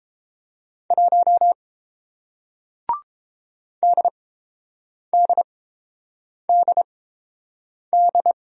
Morse code, heard as a keyed tone, about 30 wpm, spaced as for 7 wpm, 700 Hz. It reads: 1 DDDD